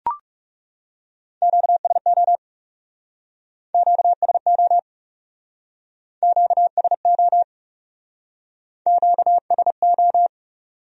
Morse code, heard as a keyed tone, about 38 wpm, 700 Hz. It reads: QSO QSO QSO QSO